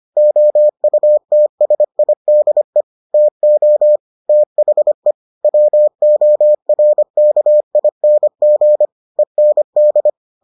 Morse code, heard as a keyed tone, about 25 wpm, 600 Hz. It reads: OUTSIDE TO THE WORKING END